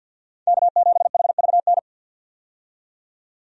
Morse code, heard as a keyed tone, about 47 wpm, 700 Hz. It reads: K7HVN
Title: USA Call Signs - 5 Characters - 50wpm